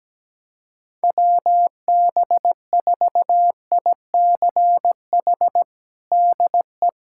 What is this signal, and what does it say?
Morse code, heard as a keyed tone, about 17 wpm, 700 Hz. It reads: WB4ICH DE
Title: USA Callsign DE Callsign - 6 Characters - 17wpm